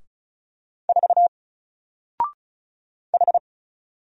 Morse code, heard as a keyed tone, about 36 wpm, 700 Hz. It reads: 4 H
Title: Single Letter-Number 35wpm